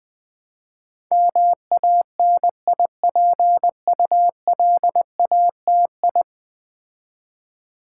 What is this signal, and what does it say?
Morse code, heard as a keyed tone, about 20 wpm, 700 Hz. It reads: MANIPULATI
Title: Build A Word - 5 Letters At A Time - 12-Letter Words - 20wpm